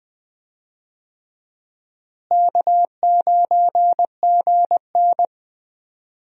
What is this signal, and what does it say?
Morse code, heard as a keyed tone, about 20 wpm, 700 Hz. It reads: K9GN
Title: USA Call Signs - 4 Characters - Repeated 3x - 20wpm